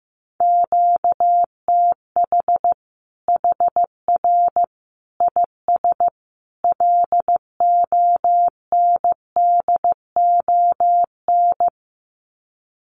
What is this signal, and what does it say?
Morse code, heard as a keyed tone, about 15 wpm, 700 Hz. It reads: QTH HR IS LONDON
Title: Common QSO Phrases 15wpm